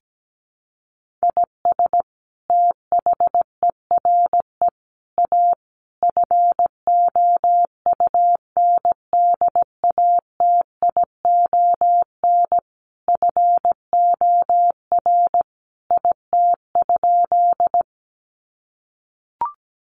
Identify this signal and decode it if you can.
Morse code, heard as a keyed tone, about 17 wpm, 700 Hz. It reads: IS THERE A FOUNDATION FOR IT?
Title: Sentences from Top 1200 Words 17wpm